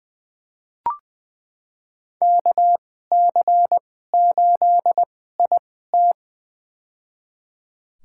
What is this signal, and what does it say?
Morse code, heard as a keyed tone, about 20 wpm, spaced as for 14 wpm, 700 Hz. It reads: KC8IT
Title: USA Call Signs - 2x Character Spacing - 20wpm